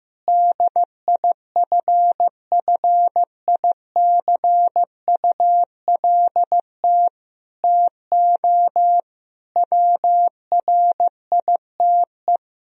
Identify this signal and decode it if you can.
Morse code, heard as a keyed tone, about 15 wpm, 700 Hz. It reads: DIFFICULT TO WRITE
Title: Sentences from Top 1000 Words - Review Part 3 - 15wpm